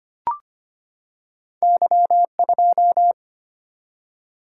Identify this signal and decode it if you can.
Morse code, heard as a keyed tone, about 25 wpm, 700 Hz. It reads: Y2
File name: Two Letter-Number Combinations - 25wpm